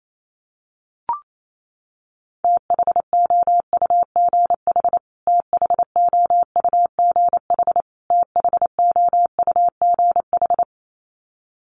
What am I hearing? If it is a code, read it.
Morse code, heard as a keyed tone, about 28 wpm, 700 Hz. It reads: THOUGH THOUGH THOUGH
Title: Top 500 Words Repeated 28wpm